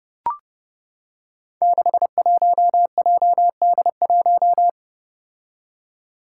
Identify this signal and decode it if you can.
Morse code, heard as a keyed tone, about 31 wpm, 700 Hz. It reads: 61JD1
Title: Five Challenging Letter-Number Combinations 30wpm